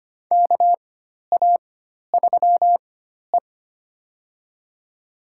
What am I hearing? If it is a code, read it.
Morse code, heard as a keyed tone, about 25 wpm, spaced as for 12 wpm, 700 Hz. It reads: KA3E
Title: USA Call Signs - 4 Characters - 4x Character Spacing - 25wpm